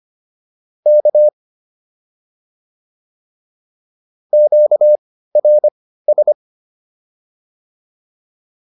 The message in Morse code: K QRS

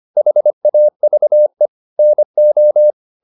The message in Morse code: HAVE NO

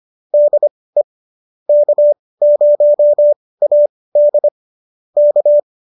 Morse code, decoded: DE K0AD K

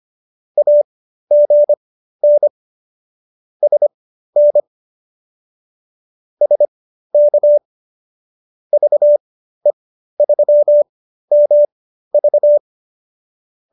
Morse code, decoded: AGN SN SK VE3MV

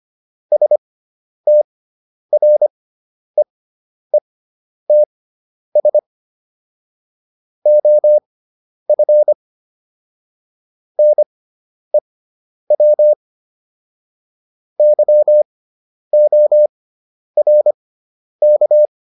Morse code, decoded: STREETS OF NEW YORK

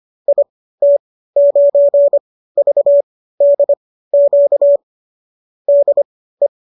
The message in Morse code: IT9VDQ DE

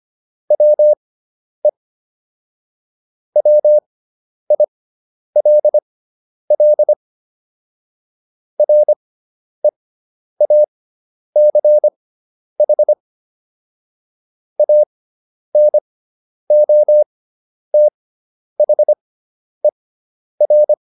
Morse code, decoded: WE WILL REACH ANOTHER

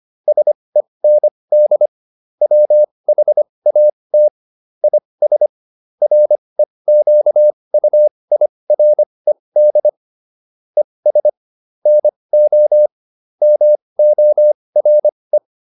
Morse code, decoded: SEND WHAT IS REQUIRED ES NO MORE

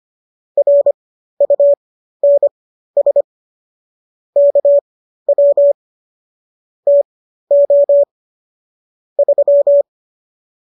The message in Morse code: RUNS KW TO 3

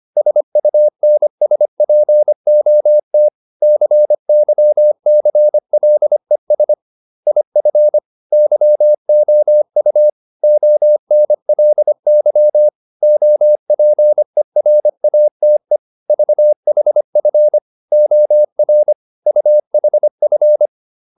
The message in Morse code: SUNSPOT CYCLES IF YOU ONLY OPERATE VHF OR UHF

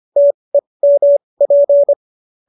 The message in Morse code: TEMP

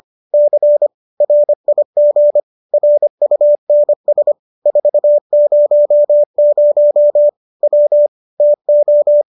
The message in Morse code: C RIG RUNS 400 W TO

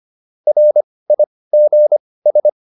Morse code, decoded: RIGS

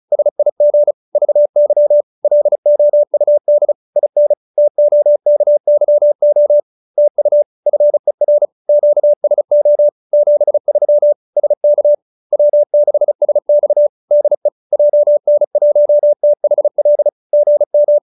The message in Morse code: SIG VY LOUD IN TOKYO TU FER QSO 73 SK W6SX DE JN1THL GM

